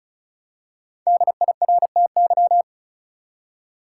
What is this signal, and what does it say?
Morse code, heard as a keyed tone, about 34 wpm, 700 Hz. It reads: DIRTY
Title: Five Letter Words - 35wpm